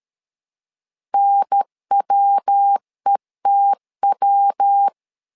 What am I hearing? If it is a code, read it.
Morse code, heard as a keyed tone, about 13 wpm, 800 Hz. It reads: NWETW
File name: LEZIONI / 3 MWRO / 3 MW 12wpm 1633